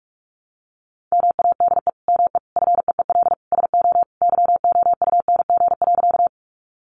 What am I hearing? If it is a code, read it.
Morse code, heard as a keyed tone, about 44 wpm, 700 Hz. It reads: MADE ME FEEL SO YOUNG.
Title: Sentences from Top 400 Words 45wpm